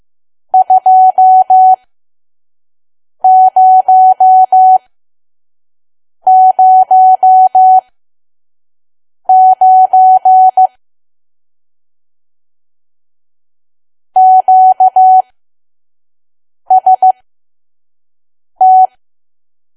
Morse code, heard as a keyed tone, about 15 wpm, spaced as for 5 wpm, 750 Hz. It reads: 2009 QST
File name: Archive / 5 WPM files / PM